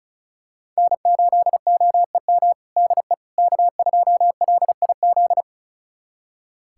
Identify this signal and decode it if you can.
Morse code, heard as a keyed tone, about 36 wpm, 700 Hz. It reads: N8OEM DE K2LIZ
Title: USA Callsign DE Callsign 35wpm